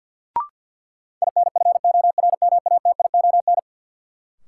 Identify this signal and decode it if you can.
Morse code, heard as a keyed tone, about 52 wpm, 700 Hz. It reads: INFORMATION
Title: Sentences from Top 600 Words - New Word Drill - 50wpm